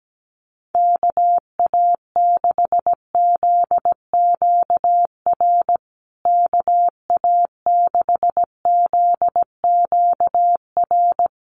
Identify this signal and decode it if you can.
Morse code, heard as a keyed tone, about 17 wpm, 700 Hz. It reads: KA6ZQR KA6ZQR
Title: USA Call Signs Repeated 3x 17wpm